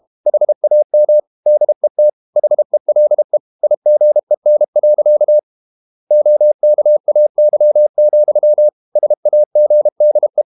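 Morse code, heard as a keyed tone, about 32 wpm, 600 Hz. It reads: HAM DET HELE IGEN. OKAY, SAGDE
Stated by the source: High-speed QRQ CW at 32 WPM.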